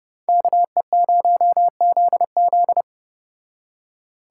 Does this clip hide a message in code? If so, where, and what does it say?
Yes: Morse code from 0.3 to 2.8 s, heard as a keyed tone, about 29 wpm, 700 Hz. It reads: KE0ZZ